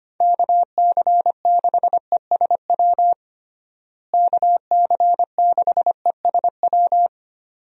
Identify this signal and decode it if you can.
Morse code, heard as a keyed tone, about 25 wpm, 700 Hz. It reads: KC6ESW KC6ESW
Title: USA Call Signs - 6 Characters - Repeated 2x - 25wpm